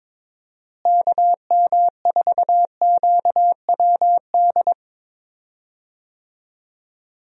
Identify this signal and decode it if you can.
Morse code, heard as a keyed tone, about 22 wpm, 700 Hz. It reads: KM4QWD
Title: USA Call Signs - 6 Characters - Repeated 3x - 22wpm